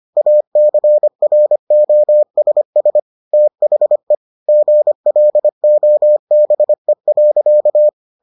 Morse code, heard as a keyed tone, about 25 wpm, 600 Hz. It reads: ACROSS THE GLOBE.